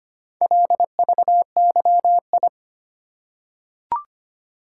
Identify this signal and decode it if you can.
Morse code, heard as a keyed tone, about 25 wpm, 700 Hz. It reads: LVYI
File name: practice / 25wpm / Four Letter-Number Combinations - 25wpm